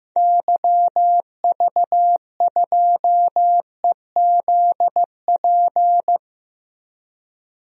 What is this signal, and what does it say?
Morse code, heard as a keyed tone, about 15 wpm, 700 Hz. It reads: YV2EZP
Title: DXCC-100 Call Signs - 6 Characters - Repeated 3x - 15wpm